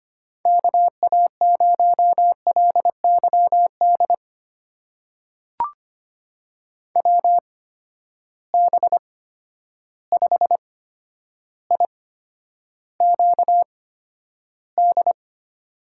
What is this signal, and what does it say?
Morse code, heard as a keyed tone, about 25 wpm, spaced as for 11 wpm, 700 Hz. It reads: KA0LYD WB5IQD
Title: USA Call Signs - 6 Characters - 8x Character Spacing - 25wpm